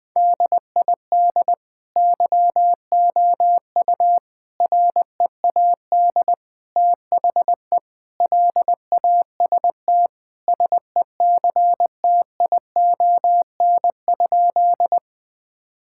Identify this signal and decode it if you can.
Morse code, heard as a keyed tone, about 19 wpm, 700 Hz. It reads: DID YOU READ THE LAST SECTION?
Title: Sentences from Top 1000 Words - Review Part 5 - 20wpm